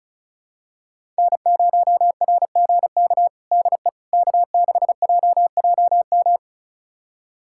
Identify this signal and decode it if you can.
Morse code, heard as a keyed tone, about 36 wpm, 700 Hz. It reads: N0RGK DE K6JJM
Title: USA Callsign DE Callsign - 5 Characters - 35wpm